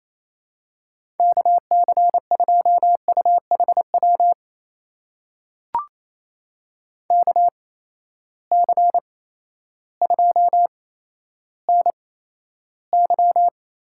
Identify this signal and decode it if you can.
Morse code, heard as a keyed tone, about 28 wpm, spaced as for 13 wpm, 700 Hz. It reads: KC2UHW KC2NY
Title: USA Call Signs - 8x Character Spacing - 28wpm